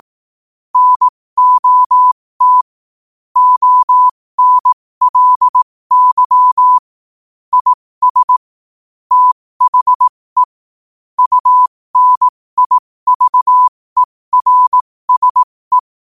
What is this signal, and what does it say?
Morse code, heard as a keyed tone, about 18 wpm, spaced as for 15 wpm, 1000 Hz. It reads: NOT ONLY IS THE UNIVERSE
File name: qotdfiles / 15wpm